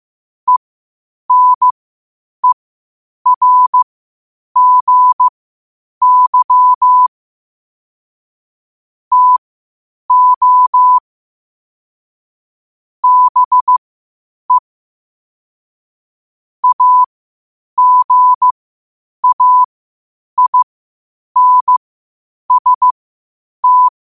Morse code, heard as a keyed tone, about 15 wpm, spaced as for 8 wpm, 1000 Hz. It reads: ENERGY TO BE AGAINST
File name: qotdfiles / 8wpm